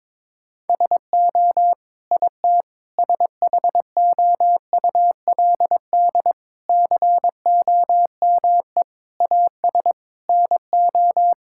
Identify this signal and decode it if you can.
Morse code, heard as a keyed tone, about 22 wpm, 700 Hz. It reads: SO IT SHOULD COME AS NO
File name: Sets of 9 Words - 22wpm